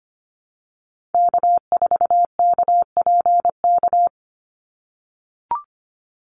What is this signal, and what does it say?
Morse code, heard as a keyed tone, about 25 wpm, 690 Hz. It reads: K4KPK
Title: NA SOTA Activator Callsigns 25wpm